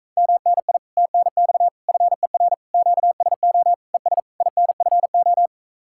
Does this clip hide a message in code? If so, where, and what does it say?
Yes: Morse code from 0.2 to 5.5 s, heard as a keyed tone, about 37 wpm, 700 Hz. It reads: MNI TNX FER QSO ES INFO